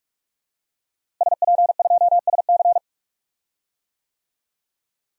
Morse code, heard as a keyed tone, about 44 wpm, 700 Hz. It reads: SP2SC
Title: DXCC-100 Call Signs - 5 Characters - Repeated 3x - 45wpm